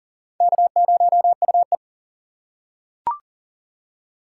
Morse code, heard as a keyed tone, about 40 wpm, 700 Hz. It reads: K0UE
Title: USA Call Signs - 4 Characters - 4x Character Spacing - 40wpm